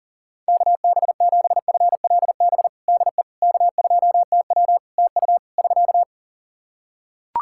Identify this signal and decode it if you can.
Morse code, heard as a keyed tone, about 40 wpm, 700 Hz. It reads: KB7FLB DE K2TW TU <SK>
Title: Practice QSO - Exchange 4 - Ending - 40wpm